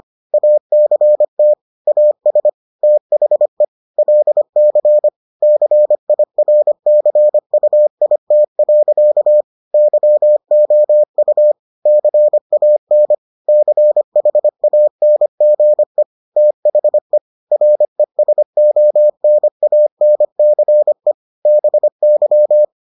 Morse code, heard as a keyed tone, about 25 wpm, 600 Hz. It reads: ACT AS THE LC CIRCUIT. YOU CAN CHANGE THE RESONANCE BY